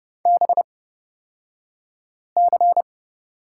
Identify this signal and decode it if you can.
Morse code, heard as a keyed tone, about 30 wpm, spaced as for 5 wpm, 700 Hz. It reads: BC
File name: Lesson 37d - Letter x - Two Characters 30wpm